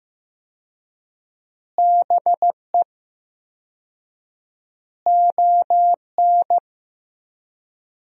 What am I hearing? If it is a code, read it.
Morse code, heard as a keyed tone, about 15 wpm, 700 Hz. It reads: BE ON
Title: 4x Word-Spacing - Sentences from Top 300 Words - 15wpm